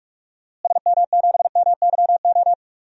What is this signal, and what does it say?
Morse code, heard as a keyed tone, about 45 wpm, 700 Hz. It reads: SM7MYO
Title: DXCC-100 Call Signs - 6 Characters - Repeated 2x - 45wpm